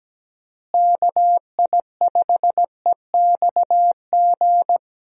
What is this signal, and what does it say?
Morse code, heard as a keyed tone, about 17 wpm, 700 Hz. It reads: KI5EXG
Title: USA Call Signs - 6 Characters - Repeated 2x - 17wpm